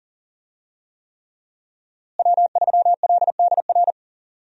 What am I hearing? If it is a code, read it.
Morse code, heard as a keyed tone, about 40 wpm, 700 Hz. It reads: W3LDR
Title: USA Call Signs - 5 Characters - Repeated 3x - 40wpm